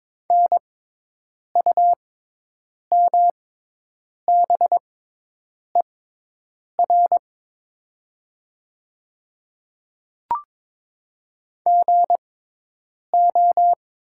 Morse code, heard as a keyed tone, about 22 wpm, spaced as for 8 wpm, 700 Hz. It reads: NUMBER GO
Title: 6x Character Spacing - Top 100 Words - 22wpm